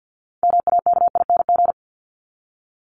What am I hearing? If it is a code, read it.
Morse code, heard as a keyed tone, about 50 wpm, 700 Hz. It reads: MAKING
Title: ING Suffix - Top 100 Words with Base Words 50wpm